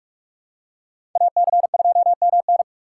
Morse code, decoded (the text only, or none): AC2MN